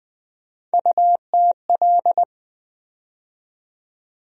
Morse code, heard as a keyed tone, about 20 wpm, 700 Hz. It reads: UTL
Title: USA Call Sign Postfixes - 3 Letters 20wpm